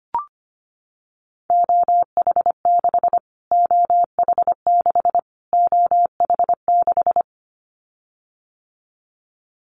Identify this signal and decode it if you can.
Morse code, heard as a keyed tone, about 25 wpm, 700 Hz. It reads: OH6 OH6 OH6
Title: Callsign Prefixes for Top 25 DXCC - Repeated 3x - 25wpm